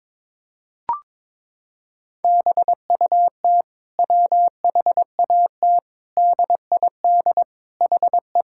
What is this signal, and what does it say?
Morse code, heard as a keyed tone, about 22 wpm, 700 Hz. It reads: BUT WHAT DID HE
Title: Sentences from Top 1000 Words - Review Part 6 - 22wpm